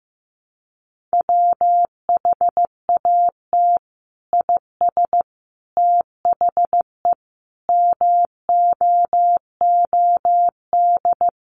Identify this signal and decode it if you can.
Morse code, heard as a keyed tone, about 15 wpm, 700 Hz. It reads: WHAT IS THE MOOD
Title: Sentences from Top 1100 Words 15wpm